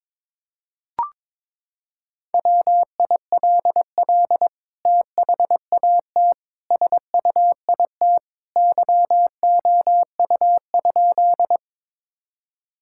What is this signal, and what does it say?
Morse code, heard as a keyed tone, about 22 wpm, 700 Hz. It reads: WILL THAT SUIT YOU?
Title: Sentences from Top 1600 Words 22wpm